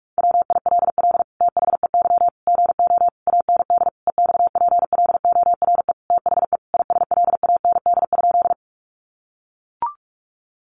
Morse code, heard as a keyed tone, about 44 wpm, 700 Hz. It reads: WILL THEY GO AND EXPLORE THE ISLAND?
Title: Sentences from Top 1100 Words 45wpm